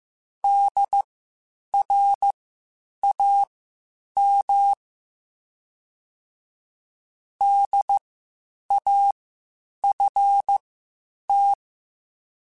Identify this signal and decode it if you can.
Morse code, heard as a keyed tone, about 15 wpm, spaced as for 8 wpm, 800 Hz. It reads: DRAM DAFT
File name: Spaced Alphas / Spaced Alphas mp3s / Ds sp @13